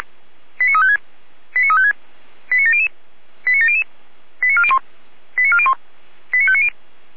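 Five-tone selective call (ZVEI), heard as quick runs of five tones, tones of about 70 ms. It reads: 89356 89356 89700 89700 89401 89401 89590
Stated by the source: ZVEI five-tone selective call 89164 received by SDR, recorded by feuerwehr-orlamuende (Orlamünde fire brigade).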